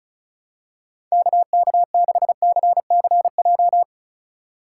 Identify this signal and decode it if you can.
Morse code, heard as a keyed tone, about 34 wpm, 700 Hz. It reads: KK6CCJ
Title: USA Call Signs - 6 Characters - 3x Character Spacing - 35wpm